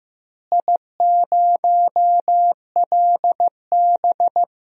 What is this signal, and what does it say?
Morse code, heard as a keyed tone, about 15 wpm, 700 Hz. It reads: I0LB
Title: DXCC-100 Call Signs - 4 Characters - Repeated 3x - 15wpm